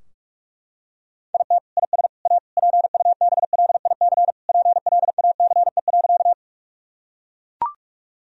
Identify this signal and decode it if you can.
Morse code, heard as a keyed tone, about 42 wpm, 700 Hz. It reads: IT IS A PUBLIC PLACE.
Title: Sentences from Top 1000 Words - Review Part 3 - 45wpm